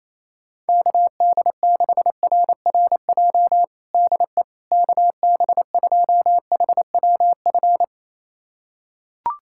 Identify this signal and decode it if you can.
Morse code, heard as a keyed tone, about 28 wpm, 700 Hz. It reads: KD6RRJ DE KB2HWF
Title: USA Callsign DE Callsign - 6 Characters - 28wpm